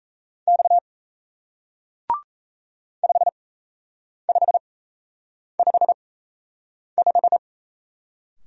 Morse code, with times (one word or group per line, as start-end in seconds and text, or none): 0.5-0.8 X
3.0-7.4 5555